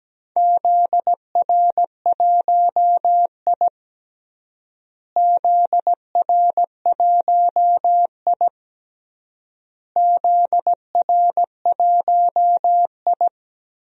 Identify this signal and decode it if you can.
Morse code, heard as a keyed tone, about 17 wpm, 700 Hz. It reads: ZR1I ZR1I ZR1I
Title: DXCC-100 Call Signs - 4 Characters - Repeated 3x - 17wpm